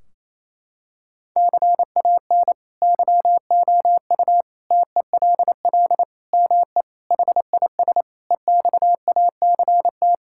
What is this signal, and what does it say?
Morse code, heard as a keyed tone, about 28 wpm, 700 Hz. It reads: CAN YOU TELL ME HIS EXACT
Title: Sentences from Top 1600 Words 28wpm